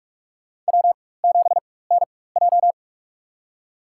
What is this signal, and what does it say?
Morse code, heard as a keyed tone, about 46 wpm, spaced as for 21 wpm, 700 Hz. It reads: W7NJ
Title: USA Call Signs - 4 Characters - 4x Character Spacing - 45wpm